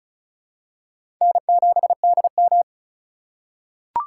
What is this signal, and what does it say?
Morse code, heard as a keyed tone, about 34 wpm, 700 Hz. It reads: N7DM
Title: USA Call Signs - 4 Characters - 4x Character Spacing - 35wpm